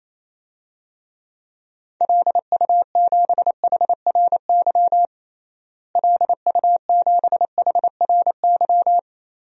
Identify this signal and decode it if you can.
Morse code, heard as a keyed tone, about 28 wpm, 700 Hz. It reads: LU7HRY LU7HRY
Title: DXCC-100 Call Signs - 6 Characters - Repeated 3x - 28wpm